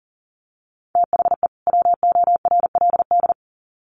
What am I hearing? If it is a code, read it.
Morse code, heard as a keyed tone, about 40 wpm, 700 Hz. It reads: THE WORLD
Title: Sets of 2 Words 40wpm